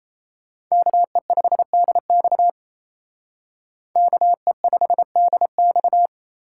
Morse code, heard as a keyed tone, about 35 wpm, 700 Hz. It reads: KE5DX KE5DX
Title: Speed Racing - USA 2x2 Call Signs 28wpm